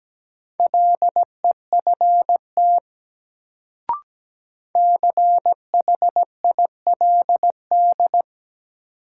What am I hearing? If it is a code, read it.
Morse code, heard as a keyed tone, about 17 wpm, 700 Hz. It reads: LEFT CHILD
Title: Sets of 5 Words Encore - New Word Drill - 17wpm